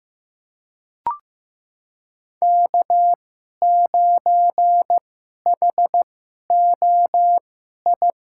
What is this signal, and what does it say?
Morse code, heard as a keyed tone, about 15 wpm, spaced as for 11 wpm, 700 Hz. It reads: K9HOI
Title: USA Call Signs - 5 Characters - 2x Character Spacing - 15wpm